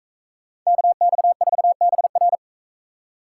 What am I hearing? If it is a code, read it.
Morse code, heard as a keyed tone, about 40 wpm, 700 Hz. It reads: KX4BR